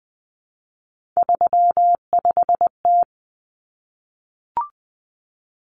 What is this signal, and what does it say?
Morse code, heard as a keyed tone, about 20 wpm, 700 Hz. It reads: 35T
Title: Common Cut Numbers - 3-Digit 20wpm